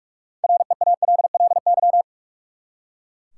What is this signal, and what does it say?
Morse code, heard as a keyed tone, about 41 wpm, 700 Hz. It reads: REALLY
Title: Six Letter Words - 45wpm